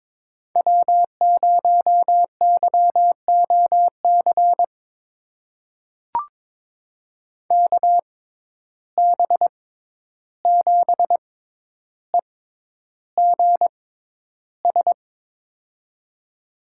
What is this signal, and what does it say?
Morse code, heard as a keyed tone, about 22 wpm, spaced as for 11 wpm, 700 Hz. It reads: W0YOC KB7EGS
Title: USA Call Signs - 6x Character Spacing - 22wpm